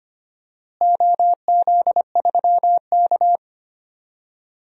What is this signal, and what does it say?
Morse code, heard as a keyed tone, about 25 wpm, 700 Hz. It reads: OZ3K
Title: DXCC-100 Call Signs - 4 Characters - Repeated 2x - 25wpm